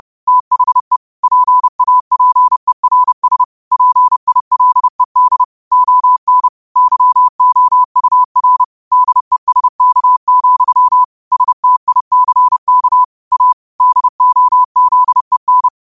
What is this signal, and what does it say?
Morse code, heard as a keyed tone, about 30 wpm, 1000 Hz. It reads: THE PAPERS PILED ON YOUR DESK, STICK A DOZEN
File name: qotdfiles / 30wpm